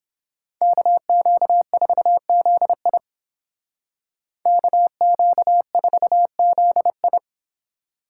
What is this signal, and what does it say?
Morse code, heard as a keyed tone, about 30 wpm, 700 Hz. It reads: KQ4ZI KQ4ZI